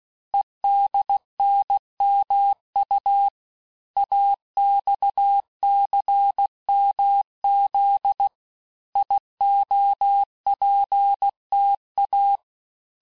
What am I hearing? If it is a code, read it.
Morse code, heard as a keyed tone, about 16 wpm, 800 Hz. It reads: EDNMU AXCMZ IOPTA
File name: cours-du-lundi files / cours-1592--teste-chiffre-vitesse-15mpm